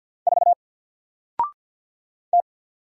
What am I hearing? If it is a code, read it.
Morse code, heard as a keyed tone, about 50 wpm, 700 Hz. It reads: 4 T
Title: Single Letter-Number Alt 50wpm